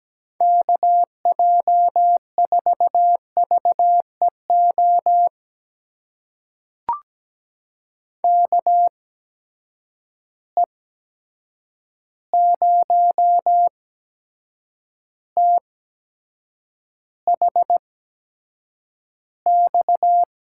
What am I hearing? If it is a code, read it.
Morse code, heard as a keyed tone, about 17 wpm, spaced as for 7 wpm, 700 Hz. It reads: KJ4VEO KE0THX
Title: USA Call Signs - 6 Characters - 8x Character Spacing - 17wpm